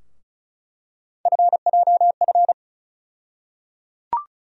Morse code, heard as a keyed tone, about 36 wpm, 700 Hz. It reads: FJF
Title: USA Call Sign Postfixes - 3 Letters 35wpm